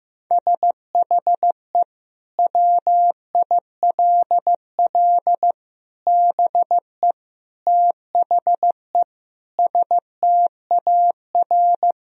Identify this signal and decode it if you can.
Morse code, heard as a keyed tone, about 15 wpm, 700 Hz. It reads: SHE WILL BE THE STAR
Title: Sentences from Top 1000 Words - Review Part 3 - 15wpm